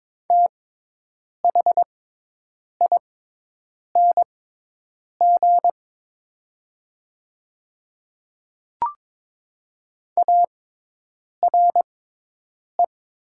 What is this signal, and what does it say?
Morse code, heard as a keyed tone, about 22 wpm, spaced as for 8 wpm, 700 Hz. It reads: THING ARE
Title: 6x Character Spacing - Top 300 Words - 22wpm